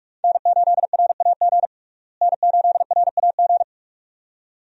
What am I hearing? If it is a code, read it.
Morse code, heard as a keyed tone, about 42 wpm, 700 Hz. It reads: N8RAG N8RAG
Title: USA Call Signs - 5 Characters - Repeated 2x - 45wpm